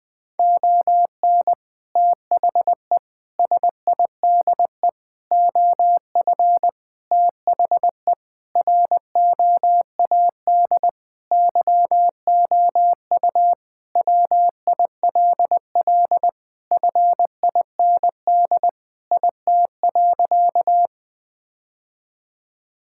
Morse code, heard as a keyed tone, about 20 wpm, 700 Hz. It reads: ON THE SIDE OF THE ROAD YOU WILL FIND IT.
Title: Sentences with 3-Word Phrases - Spoken Beginning - 20wpm